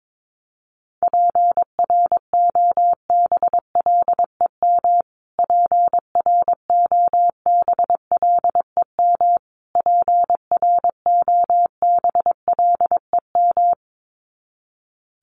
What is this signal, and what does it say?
Morse code, heard as a keyed tone, about 22 wpm, 700 Hz. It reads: PROBLEM PROBLEM PROBLEM
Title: Top 500 Words Repeated 22wpm